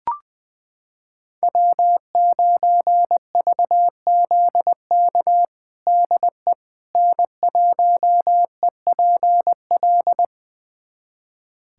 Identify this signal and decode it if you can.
Morse code, heard as a keyed tone, about 20 wpm, 700 Hz. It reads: W9VZK DE N1EPL
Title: USA Callsign DE Callsign - 5 Characters - 20wpm